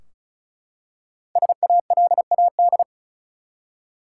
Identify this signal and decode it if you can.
Morse code, heard as a keyed tone, about 36 wpm, 700 Hz. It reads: SALAD
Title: Five Letter Words - 35wpm